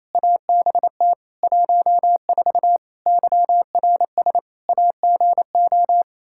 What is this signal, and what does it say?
Morse code, heard as a keyed tone, about 27 wpm, 700 Hz. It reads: ABT 14 YRS AGO
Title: Practice QSO - Exchange 3 - More Info - 28wpm